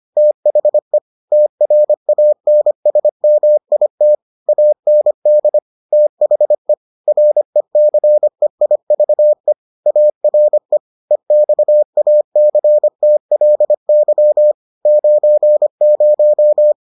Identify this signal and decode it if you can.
Morse code, heard as a keyed tone, about 25 wpm, 600 Hz. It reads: THE TRANSMIT AND THE RECEIVE ARE EXACTLY 90